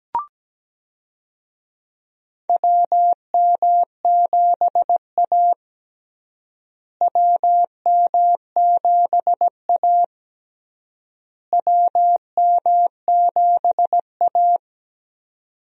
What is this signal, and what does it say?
Morse code, heard as a keyed tone, about 17 wpm, 700 Hz. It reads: WM7A WM7A WM7A
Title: USA Call Signs - 4 Characters - Repeated 3x - 17wpm